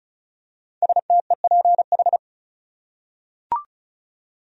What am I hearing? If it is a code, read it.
Morse code, heard as a keyed tone, about 36 wpm, 700 Hz. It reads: STEPH
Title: USA Common Names - Max 5 Letters - 35wpm